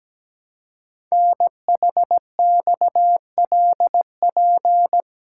Morse code, heard as a keyed tone, about 17 wpm, 700 Hz. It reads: NHXLP
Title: Code Groups - 5 Letter:Numbers - 17wpm